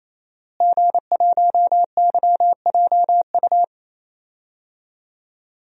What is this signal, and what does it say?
Morse code, heard as a keyed tone, about 28 wpm, 690 Hz. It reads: G1YJU
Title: DXCC-100 Call Signs - 5 Characters - Repeated 3x - 28wpm